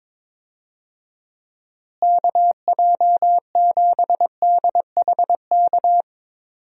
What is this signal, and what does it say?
Morse code, heard as a keyed tone, about 22 wpm, 700 Hz. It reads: KJ7DHK
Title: USA Call Signs - 6 Characters - Repeated 2x - 22wpm